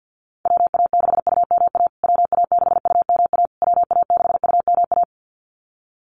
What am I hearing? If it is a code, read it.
Morse code, heard as a keyed tone, about 50 wpm, 700 Hz. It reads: WA6UMA WA6UMA WA6UMA
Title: USA Call Signs Repeated 3x 50wpm